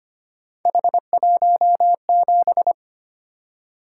Morse code, heard as a keyed tone, about 25 wpm, 700 Hz. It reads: H17
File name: Three Letter-Number Combinations - 25wpm